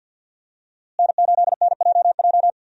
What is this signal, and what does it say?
Morse code, heard as a keyed tone, about 51 wpm, 700 Hz. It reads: N8NJJ
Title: USA Call Signs - 5 Characters - Repeated 2x - 50wpm